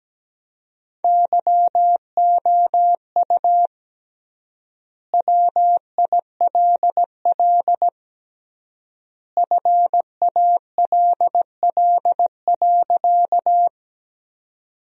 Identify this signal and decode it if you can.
Morse code, heard as a keyed tone, about 17 wpm, 700 Hz. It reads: YOU WILL FALL.
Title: 3x Word-Spacing - Sentences from Top 500 Words - 17wpm